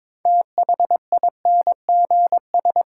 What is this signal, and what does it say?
Morse code, heard as a keyed tone, about 23 wpm, 700 Hz. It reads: THINGS
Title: Sentences from Top 1400 Words - New Word Drill Alt - 22wpm